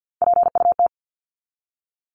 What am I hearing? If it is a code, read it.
Morse code, heard as a keyed tone, about 50 wpm, 700 Hz. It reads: PUT
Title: Top 500 Words 50wpm